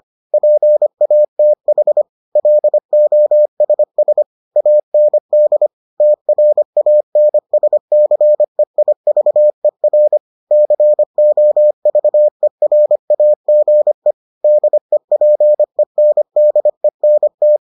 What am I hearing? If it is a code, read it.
Morse code, heard as a keyed tone, about 25 wpm, 600 Hz. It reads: PATH LOSS AND TRANSCEIVER COVERAGE DEPENDENT